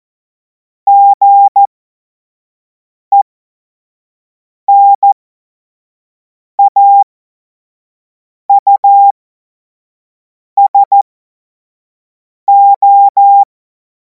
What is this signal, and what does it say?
Morse code, heard as a keyed tone, about 14 wpm, spaced as for 5 wpm, 800 Hz. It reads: GENAUSO
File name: Morsekurs / Witze / Witz 09. Juni 2026 25 F70